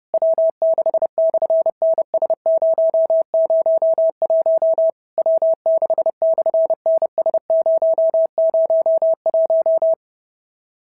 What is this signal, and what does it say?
Morse code, heard as a keyed tone, about 31 wpm, 650 Hz. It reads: W6/NS001 W6/NS001
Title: SOTA QSO Practice 30wpm